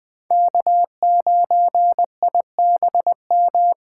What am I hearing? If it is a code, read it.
Morse code, heard as a keyed tone, about 20 wpm, 700 Hz. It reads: K9IBM
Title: K1USN SST QSO Practice - 20wpm